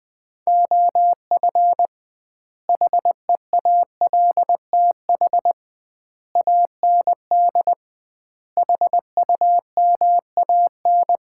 Morse code, heard as a keyed tone, about 20 wpm, 700 Hz. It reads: OF HEALTH AND HUMAN
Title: 2x Word-Spacing - Sets of 5 Words - 20wpm